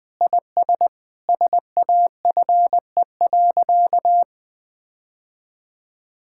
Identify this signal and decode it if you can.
Morse code, heard as a keyed tone, about 21 wpm, 700 Hz. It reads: IS SAFE.
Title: Sentences with 6-Word Phrases - Spoken Beginning - 20wpm